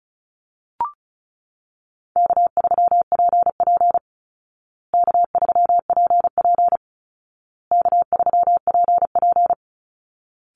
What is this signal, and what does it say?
Morse code, heard as a keyed tone, about 34 wpm, 700 Hz. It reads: K3PP K3PP K3PP
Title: Easy Call Signs Repeated 3x - 35wpm